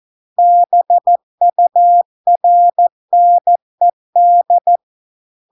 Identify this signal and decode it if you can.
Morse code, heard as a keyed tone, about 14 wpm, 700 Hz. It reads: BURNED